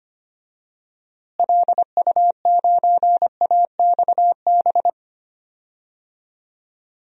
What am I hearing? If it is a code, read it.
Morse code, heard as a keyed tone, about 25 wpm, 700 Hz. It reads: LU9AXB
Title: DXCC-100 Call Signs - 6 Characters - Repeated 2x - 25wpm